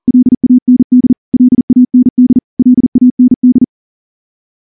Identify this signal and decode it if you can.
Morse code, heard as a keyed tone, about 38 wpm, 250 Hz. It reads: LAND LAND LAND
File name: morse-training-40WPM